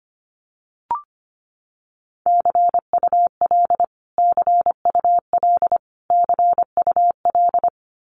Morse code, heard as a keyed tone, about 25 wpm, 700 Hz. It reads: CUL CUL CUL
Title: Abbreviations Repeated 3x - 25wpm